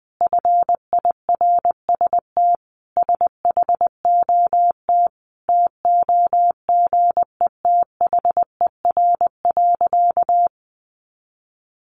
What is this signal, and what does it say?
Morse code, heard as a keyed tone, about 20 wpm, 700 Hz. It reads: FIRST SHOT TOGETHER.
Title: Sentences from Top 1100 Words 20wpm